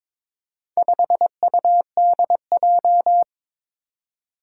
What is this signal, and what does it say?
Morse code, heard as a keyed tone, about 22 wpm, 700 Hz. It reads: 5UDJ
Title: Four Challenging Letter-Number Combinations 22wpm